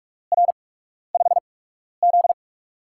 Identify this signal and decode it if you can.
Morse code, heard as a keyed tone, about 44 wpm, spaced as for 12 wpm, 700 Hz. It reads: R5Z
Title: USA Call Signs - 8x Character Spacing - 45wpm